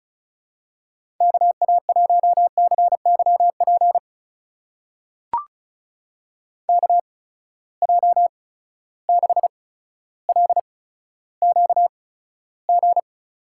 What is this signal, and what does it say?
Morse code, heard as a keyed tone, about 36 wpm, spaced as for 15 wpm, 700 Hz. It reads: KA1CYP KJ6LQG
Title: USA Call Signs - 6 Characters - 8x Character Spacing - 35wpm